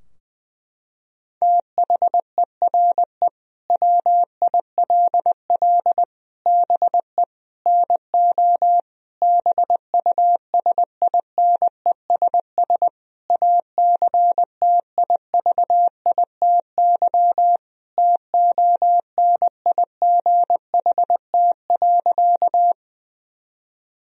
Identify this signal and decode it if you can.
Morse code, heard as a keyed tone, about 19 wpm, 700 Hz. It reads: THERE WILL BE NO BUSINESS ACTIVITY TONIGHT.
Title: Sentences from Top 1000 Words - Review Part 4 - 20wpm